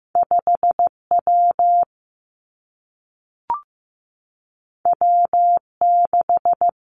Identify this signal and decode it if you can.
Morse code, heard as a keyed tone, about 15 wpm, 700 Hz. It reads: 5 W W 6
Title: Two Challenging Letter-Number Combinations 15wpm